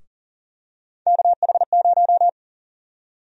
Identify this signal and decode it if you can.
Morse code, heard as a keyed tone, about 40 wpm, 700 Hz. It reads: KH0
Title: USA Call Sign Prefixes - 40wpm